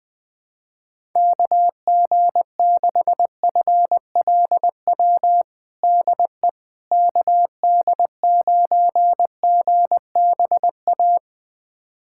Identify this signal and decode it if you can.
Morse code, heard as a keyed tone, about 20 wpm, 700 Hz. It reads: KG6FLW DE KD9GBA
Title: USA Callsign DE Callsign 20wpm